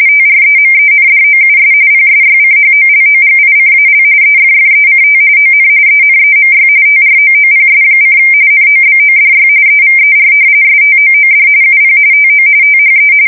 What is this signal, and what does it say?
RTTY, about 45.45 baud. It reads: TREMENDOUS PSYCHOLOGICAL EFFECT ON ENEMY TROOPS: NOBODY EVER STOOD UP TO THE
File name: fsk170 / battleshipnj